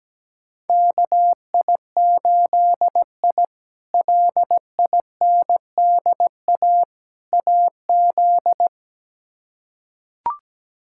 Morse code, heard as a keyed tone, about 17 wpm, 700 Hz. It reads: KI8I LINDA AZ
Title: K1USN SST QSO Practice - 17wpm